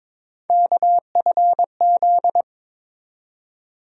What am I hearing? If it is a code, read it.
Morse code, heard as a keyed tone, about 21 wpm, 700 Hz. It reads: KFZ
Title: USA Call Sign Postfixes - 3 Letters 22wpm